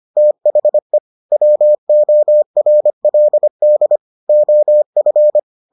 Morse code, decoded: THE WORLD OF

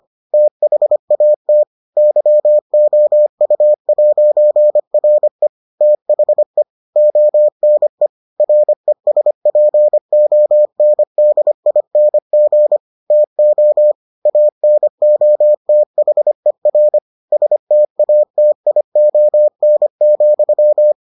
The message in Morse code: THAT YOU'RE THE ONE RESPONDING TO ANOTHER STATION,

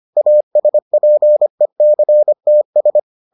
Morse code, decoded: ASPECTS

